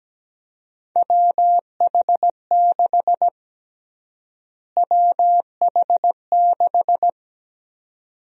Morse code, heard as a keyed tone, about 17 wpm, 700 Hz. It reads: WH6 WH6
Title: USA Call Sign Prefixes - Repeated 2x - 17wpm